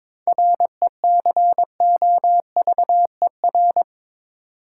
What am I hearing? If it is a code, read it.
Morse code, heard as a keyed tone, about 22 wpm, 700 Hz. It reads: RECOVER
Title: Seven Letter Words - 22wpm